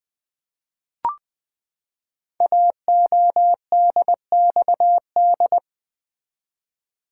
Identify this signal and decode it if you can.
Morse code, heard as a keyed tone, about 20 wpm, 700 Hz. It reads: AODXD
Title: Code Groups - 5 Letters - 20wpm